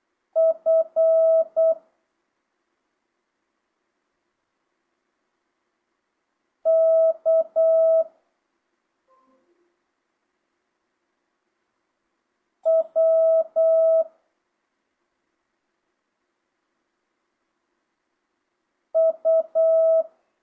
Morse code, heard as a keyed tone, about 8 wpm, 650 Hz. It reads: F K W U
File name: jh8cbh / 2025cw